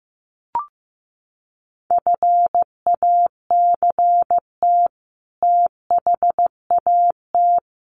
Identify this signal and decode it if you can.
Morse code, heard as a keyed tone, about 15 wpm, 700 Hz. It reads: FACT THAT
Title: Sets of 2 Words 15wpm